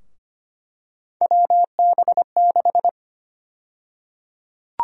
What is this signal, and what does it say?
Morse code, heard as a keyed tone, about 25 wpm, 700 Hz. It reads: WB6
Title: USA Call Sign Prefixes - 25wpm